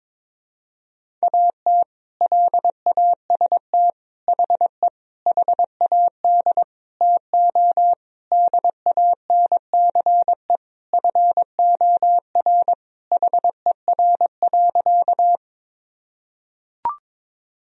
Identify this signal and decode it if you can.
Morse code, heard as a keyed tone, about 22 wpm, 700 Hz. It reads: AT LAST HE HAD TO DANCE FOR HER.
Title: Sentences from Top 1000 Words - Review Part 4 - 22wpm